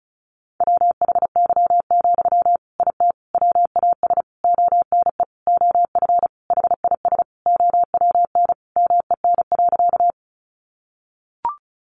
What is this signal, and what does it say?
Morse code, heard as a keyed tone, about 34 wpm, 700 Hz. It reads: WHY, IT WAS ONE OF HIS OWN MEN.
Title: Sentences from Top 300 Words 35wpm